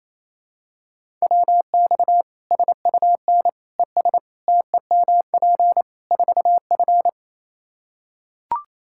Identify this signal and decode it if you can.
Morse code, heard as a keyed tone, about 28 wpm, 700 Hz. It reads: WX SUN ES TEMP 4F
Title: Practice QSO - Exchange 2 - Info - 28wpm